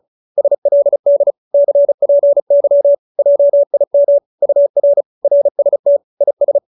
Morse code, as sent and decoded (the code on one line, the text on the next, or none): ... .-.. -.. / -.-. .--. -.-- / .--- .. -- / ..- .-. / .-. ... - / .. ...
SLD CPY JIM UR RST IS